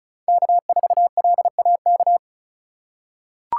Morse code, heard as a keyed tone, about 34 wpm, 700 Hz. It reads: K4LAK
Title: USA Call Signs - 5 Characters - 3x Character Spacing - 35wpm